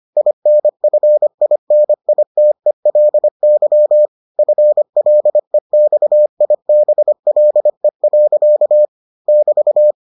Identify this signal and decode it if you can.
Morse code, heard as a keyed tone, about 25 wpm, 600 Hz. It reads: INFINITELY FLEXIBLE. <BT>